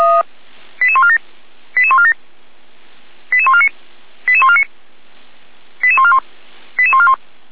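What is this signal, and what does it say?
Five-tone selective call (ZVEI), heard as quick runs of five tones, tones of about 70 ms. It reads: 88147 88147 88149 88149 88141 88141
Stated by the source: ZVEI five-tone selective call received by SDR, from the Feuerwehr Orlamünde (fire brigade) audio server.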